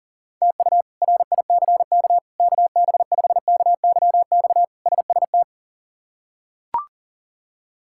Morse code, heard as a keyed tone, about 40 wpm, 700 Hz. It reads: TU RICK KB5KYX SST
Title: K1USN SST QSO Practice - 40wpm